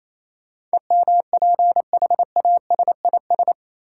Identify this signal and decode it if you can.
Morse code, heard as a keyed tone, about 28 wpm, 700 Hz. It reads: EMPHASIS
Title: Sentences from Top 1400 Words - New Word Drill - 28wpm